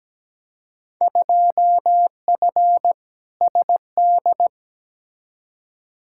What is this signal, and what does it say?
Morse code, heard as a keyed tone, about 17 wpm, 700 Hz. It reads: 2F SD
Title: Field Day Exchanges Encore - 17wpm